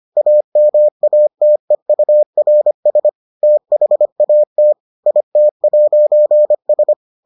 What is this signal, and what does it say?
Morse code, heard as a keyed tone, about 25 wpm, 600 Hz. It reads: AMATEURS THAT IT'S